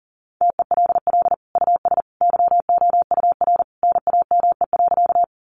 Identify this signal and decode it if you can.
Morse code, heard as a keyed tone, about 40 wpm, 700 Hz. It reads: TELL US YOUR NAME.
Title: Sentences from Top 200 Words 40wpm